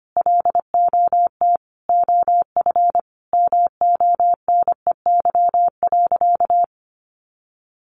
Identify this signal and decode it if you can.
Morse code, heard as a keyed tone, about 25 wpm, 700 Hz. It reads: LOT OF MONEY.
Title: Easy Phrases 25wpm